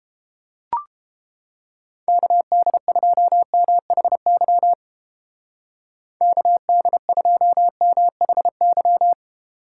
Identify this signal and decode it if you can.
Morse code, heard as a keyed tone, about 29 wpm, 700 Hz. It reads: KD2MHY KD2MHY